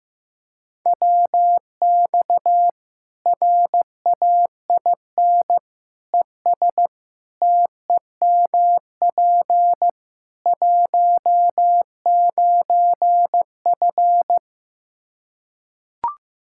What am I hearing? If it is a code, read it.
Morse code, heard as a keyed tone, about 15 wpm, 700 Hz. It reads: WX RAIN ES TEMP 19F
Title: Practice QSO - Exchange 2 - Info - 15wpm